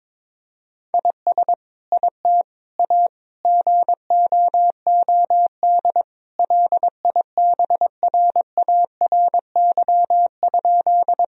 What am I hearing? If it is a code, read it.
Morse code, heard as a keyed tone, about 22 wpm, 700 Hz. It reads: IS IT A GOOD LIBRARY?
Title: Sentences from Top 1000 Words - Review Part 4 - 22wpm